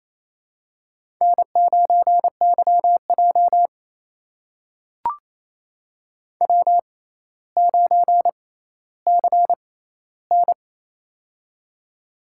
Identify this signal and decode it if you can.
Morse code, heard as a keyed tone, about 28 wpm, spaced as for 14 wpm, 700 Hz. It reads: N9YJ W9CN
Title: USA Call Signs - 6x Character Spacing - 28wpm